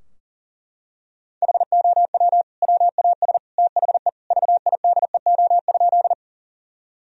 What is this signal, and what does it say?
Morse code, heard as a keyed tone, about 40 wpm, 700 Hz. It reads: HOW WAS THE VIDEO?
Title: Sentences from Top 1000 Words - Review Part 4 - 40wpm